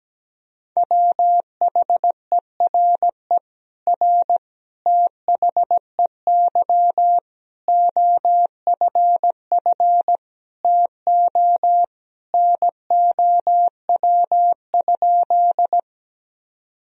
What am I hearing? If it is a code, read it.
Morse code, heard as a keyed tone, about 17 wpm, 700 Hz. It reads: WHERE R THEY OFF TO NOW?
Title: Sentences from Top 200 Words and Abbreviations 17wpm